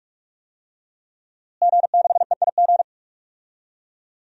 Morse code, heard as a keyed tone, about 45 wpm, 700 Hz. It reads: G6EIG
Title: DXCC-100 Call Signs - 5 Characters - Repeated 3x - 45wpm